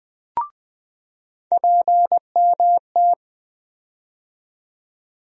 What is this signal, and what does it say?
Morse code, heard as a keyed tone, about 20 wpm, 700 Hz. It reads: PMT
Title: USA Call Sign Postfixes - 3 Letters 20wpm